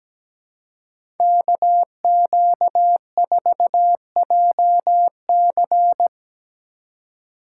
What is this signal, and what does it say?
Morse code, heard as a keyed tone, about 17 wpm, 700 Hz. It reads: KQ4JC
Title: USA Call Signs - 2x Character Spacing - 17wpm